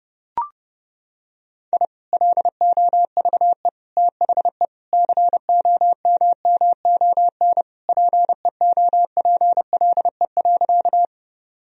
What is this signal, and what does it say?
Morse code, heard as a keyed tone, about 32 wpm, 700 Hz. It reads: I LOVE THE COMMON PEOPLE.
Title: Sentences from Top 1000 Words - Review Part 3 - 30wpm